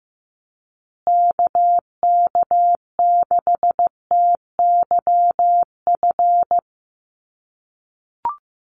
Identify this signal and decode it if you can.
Morse code, heard as a keyed tone, about 15 wpm, 700 Hz. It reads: KK6TYF
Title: USA Call Signs Repeated 2x 15wpm